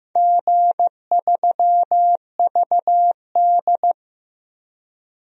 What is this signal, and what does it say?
Morse code, heard as a keyed tone, about 15 wpm, 700 Hz. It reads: G3VD
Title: DXCC-100 Call Signs - 4 Characters - 15wpm